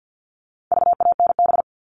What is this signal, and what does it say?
Morse code, heard as a keyed tone, about 50 wpm, 700 Hz. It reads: VANB
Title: All Cut Numbers - 4-Digit 50wpm